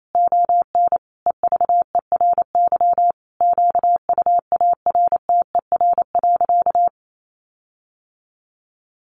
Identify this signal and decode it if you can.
Morse code, heard as a keyed tone, about 28 wpm, 700 Hz. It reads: ON EVERY QUARTER.
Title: Sentences from Top 1500 Words 28wpm